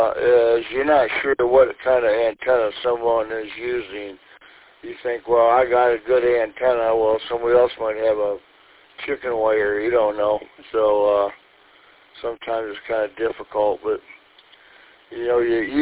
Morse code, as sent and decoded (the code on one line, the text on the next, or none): none